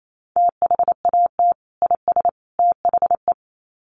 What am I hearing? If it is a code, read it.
Morse code, heard as a keyed tone, about 28 wpm, 700 Hz. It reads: THAT IS THE
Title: Sets of 3 Words 28wpm